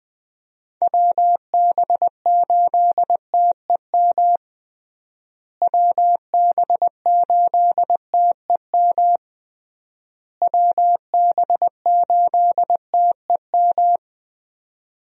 Morse code, heard as a keyed tone, about 20 wpm, 700 Hz. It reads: WB8TEM WB8TEM WB8TEM
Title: USA Call Signs - 6 Characters - Repeated 3x - 20wpm